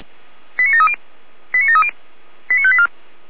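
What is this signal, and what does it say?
Five-tone selective call (ZVEI), heard as quick runs of five tones, tones of about 70 ms. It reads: 89730 89730 89584
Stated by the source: ZVEI five-tone selective call received on an SDR.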